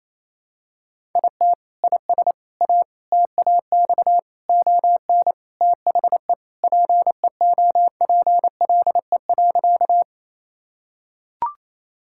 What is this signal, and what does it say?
Morse code, heard as a keyed tone, about 28 wpm, 700 Hz. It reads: IT IS A TAX ON THE PEOPLE.
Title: Sentences from Top 1000 Words - Review Part 5 - 28wpm